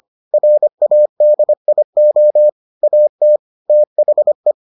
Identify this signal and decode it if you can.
Morse code, heard as a keyed tone, about 25 wpm, 600 Hz. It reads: RADIO AT THE